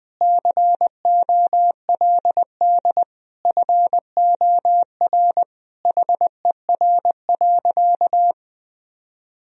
Morse code, heard as a keyed tone, about 20 wpm, 700 Hz. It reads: COLD FOR HER.
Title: Sentences from Top 1000 Words - Review Part 4 - 20wpm